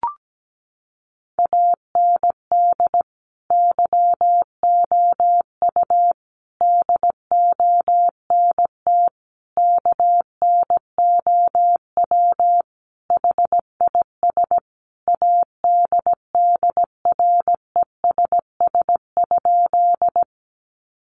Morse code, heard as a keyed tone, about 17 wpm, 700 Hz. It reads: AND YOU DONT KNOW HIS ADDRESS?
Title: Sentences from Top 1000 Words 17wpm